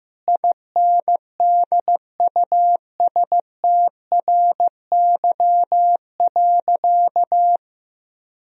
Morse code, heard as a keyed tone, about 15 wpm, 700 Hz. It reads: INDUSTRY.
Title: Sentences from Top 1000 Words - Review Part 4 - 15wpm